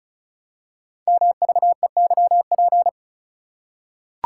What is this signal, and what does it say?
Morse code, heard as a keyed tone, about 36 wpm, 700 Hz. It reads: MVEYP
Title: Code Groups - 5 Letters - 35wpm